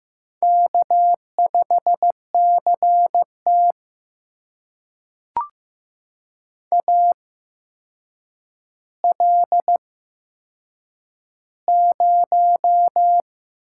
Morse code, heard as a keyed tone, about 15 wpm, spaced as for 7 wpm, 700 Hz. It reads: K5CT AL0
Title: USA Call Signs - 4 Characters - 8x Character Spacing - 15wpm